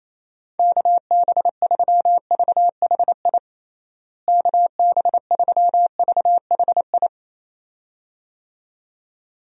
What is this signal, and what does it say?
Morse code, heard as a keyed tone, about 28 wpm, 700 Hz. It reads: KB3VHI KB3VHI
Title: USA Call Signs - 6 Characters - Repeated 2x - 28wpm